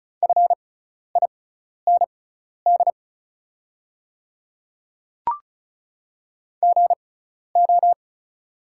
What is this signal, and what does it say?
Morse code, heard as a keyed tone, about 34 wpm, spaced as for 12 wpm, 700 Hz. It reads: FIND GO
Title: 6x Character Spacing - Top 100 Words - 35wpm